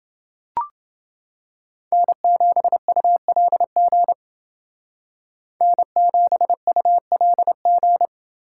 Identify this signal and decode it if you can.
Morse code, heard as a keyed tone, about 28 wpm, 700 Hz. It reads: N7ULG N7ULG